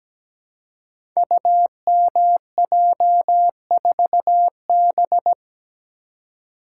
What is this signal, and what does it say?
Morse code, heard as a keyed tone, about 17 wpm, 700 Hz. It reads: UMJ4B
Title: Code Groups - 5 Letter:Numbers - 17wpm